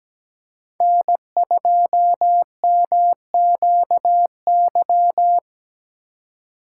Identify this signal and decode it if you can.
Morse code, heard as a keyed tone, about 17 wpm, 700 Hz. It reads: N2MQY
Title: USA Call Signs - 5 Characters - 2x Character Spacing - 17wpm